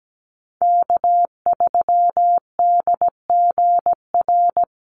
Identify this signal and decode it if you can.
Morse code, heard as a keyed tone, about 17 wpm, 700 Hz. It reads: K3DGR
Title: Call Signs - 17wpm